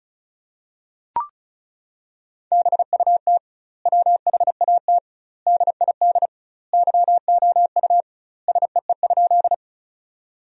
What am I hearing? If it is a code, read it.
Morse code, heard as a keyed tone, about 36 wpm, 700 Hz. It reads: BUT WHAT DID YOU SEE?
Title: 2x Word-Spacing - Sentences from Top 100 Words - 35wpm